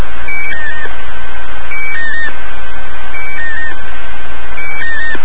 Emergency Alert System (SAME) tones, heard as harsh buzzing bursts. It is heard from 0.9 to 4.5 s.